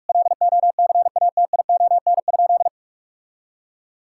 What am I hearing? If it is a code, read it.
Morse code, heard as a keyed tone, about 42 wpm, 700 Hz. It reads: LOCATION?
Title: Sentences from Top 1000 Words - Review Part 6 - 45wpm